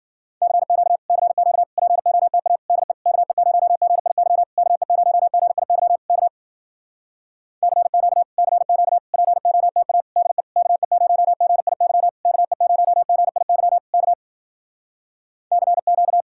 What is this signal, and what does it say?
Morse code, heard as a keyed tone, about 53 wpm, 700 Hz. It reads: CQ CQ POTA DE KE0GIY KE0GIY K CQ CQ POTA DE KE0GIY KE0GIY K CQ